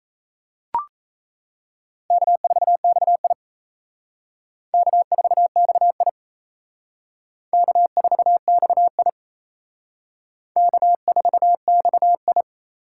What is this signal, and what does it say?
Morse code, heard as a keyed tone, about 35 wpm, 700 Hz. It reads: K4XI K4XI K4XI K4XI
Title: Speed Racing - USA 1x2 Call Signs 28wpm